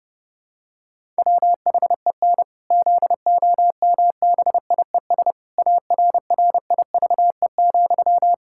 Morse code, heard as a keyed tone, about 30 wpm, 700 Hz. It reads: WHEN ZOMBIES ARRIVE,
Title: Pangram Challenge v2 - 30wpm